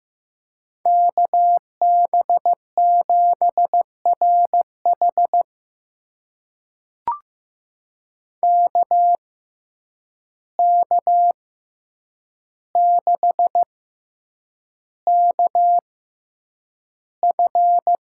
Morse code, heard as a keyed tone, about 15 wpm, spaced as for 8 wpm, 700 Hz. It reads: KB7RH KK6KF
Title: USA Call Signs - 6x Character Spacing - 15wpm